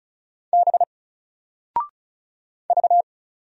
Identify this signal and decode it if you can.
Morse code, heard as a keyed tone, about 36 wpm, 700 Hz. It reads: B V
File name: Single-Letter-Number-35wpm